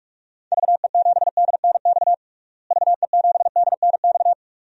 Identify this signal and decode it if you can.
Morse code, heard as a keyed tone, about 42 wpm, 700 Hz. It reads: VE7DNX VE7DNX
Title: DXCC-100 Call Signs - 6 Characters - Repeated 2x - 45wpm